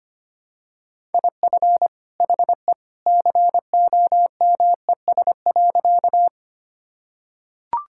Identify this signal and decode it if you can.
Morse code, heard as a keyed tone, about 25 wpm, 700 Hz. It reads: IF HE COMES.
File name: practice / 25wpm / Sentences with 2-Word Phrases - 25wpm